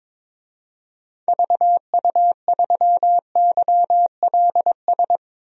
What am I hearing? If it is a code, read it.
Morse code, heard as a keyed tone, about 22 wpm, 700 Hz. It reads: VU3YLS
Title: DXCC-100 Call Signs - 6 Characters - 22wpm